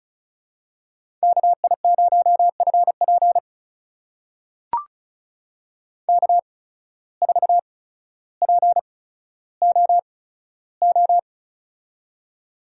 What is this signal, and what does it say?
Morse code, heard as a keyed tone, about 36 wpm, spaced as for 15 wpm, 700 Hz. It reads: KI0FP K4POO
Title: USA Call Signs - 5 Characters - 8x Character Spacing - 35wpm